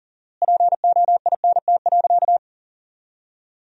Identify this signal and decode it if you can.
Morse code, heard as a keyed tone, about 40 wpm, 700 Hz. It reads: POINT.
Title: 4x Word-Spacing - Sentences from Top 200 Words - 40wpm